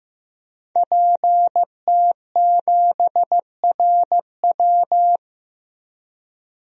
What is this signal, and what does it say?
Morse code, heard as a keyed tone, about 15 wpm, 700 Hz. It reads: PT7RW
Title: DXCC-100 Call Signs - 5 Characters - 15wpm